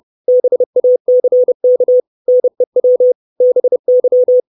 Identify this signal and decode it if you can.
Morse code, heard as a keyed tone, about 30 wpm, 500 Hz. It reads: BACK NEW BY
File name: 632 Words 30WPM